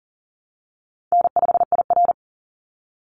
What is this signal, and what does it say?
Morse code, heard as a keyed tone, about 40 wpm, 700 Hz. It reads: N5IR
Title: Easy Call Signs Repeated 2x - 40wpm